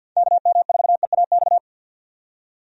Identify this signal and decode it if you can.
Morse code, heard as a keyed tone, about 50 wpm, 700 Hz. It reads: KM4EAX
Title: USA Call Signs - 6 Characters - 50wpm